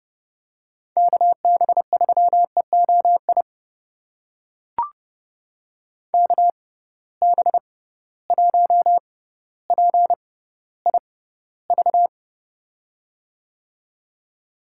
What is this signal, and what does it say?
Morse code, heard as a keyed tone, about 31 wpm, spaced as for 16 wpm, 700 Hz. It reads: KB3EOI KB1PIV
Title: USA Call Signs - 6 Characters - 6x Character Spacing - 30wpm